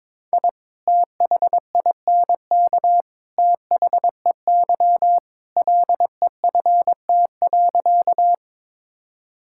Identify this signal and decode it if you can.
Morse code, heard as a keyed tone, about 21 wpm, 700 Hz. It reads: I THINK THEY LEFT.
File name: practice / 22wpm / Sentences with 3-Word Phrases - 22wpm